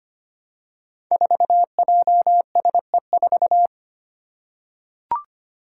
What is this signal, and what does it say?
Morse code, heard as a keyed tone, about 25 wpm, 700 Hz. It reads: 4JSE4
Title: Five Challenging Letter-Number Combinations 25wpm